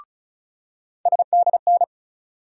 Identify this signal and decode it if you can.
Morse code, heard as a keyed tone, about 34 wpm, 700 Hz. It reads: SDN
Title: USA Call Sign Postfixes - 3 Letters 35wpm